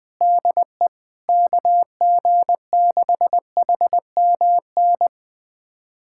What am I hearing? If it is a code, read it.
Morse code, heard as a keyed tone, about 21 wpm, 700 Hz. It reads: DE KG6HMN
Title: USA Callsign DE Callsign 20wpm